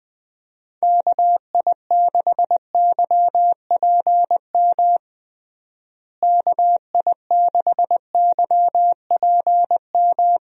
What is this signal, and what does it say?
Morse code, heard as a keyed tone, about 20 wpm, 700 Hz. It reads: KI6YPM KI6YPM
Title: USA Call Signs - 6 Characters - Repeated 3x - 20wpm